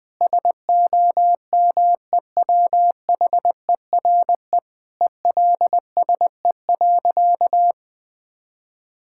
Morse code, heard as a keyed tone, about 21 wpm, 700 Hz. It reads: SOMEWHERE ELSE.
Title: Sentences with 5-Word Phrases - 20wpm